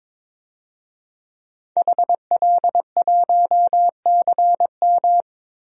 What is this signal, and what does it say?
Morse code, heard as a keyed tone, about 22 wpm, 700 Hz. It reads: HL1CM
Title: DXCC-100 Call Signs - 5 Characters - 22wpm